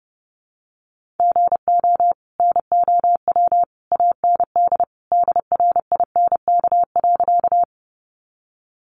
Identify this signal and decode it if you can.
Morse code, heard as a keyed tone, about 28 wpm, 700 Hz. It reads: GO NOW AND DRINK.
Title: Sentences from Top 1500 Words 30wpm